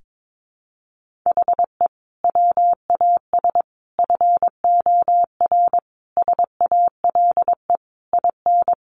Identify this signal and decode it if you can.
Morse code, heard as a keyed tone, about 22 wpm, 700 Hz. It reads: HE WAS FOR SALE IN
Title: Sentences from Top 1400 Words 22wpm